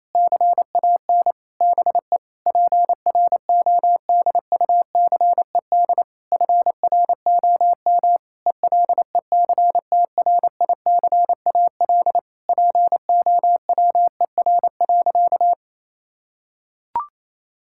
Morse code, heard as a keyed tone, about 27 wpm, 700 Hz. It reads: CAN BE PRODUCED FROM ELECTRICAL POWER.
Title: Sentences from Top 1000 Words - Review Part 6 - 28wpm